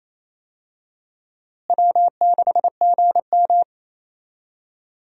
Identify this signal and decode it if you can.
Morse code, heard as a keyed tone, about 28 wpm, 700 Hz. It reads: W6GM
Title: USA Call Signs - 4 Characters - Repeated 2x - 28wpm